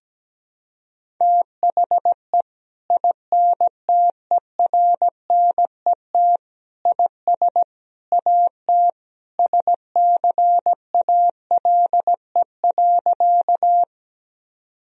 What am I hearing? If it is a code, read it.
Morse code, heard as a keyed tone, about 17 wpm, 700 Hz. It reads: THE INTERNET IS AT SCALE.
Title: Sentences from Top 1000 Words - Review Part 4 - 17wpm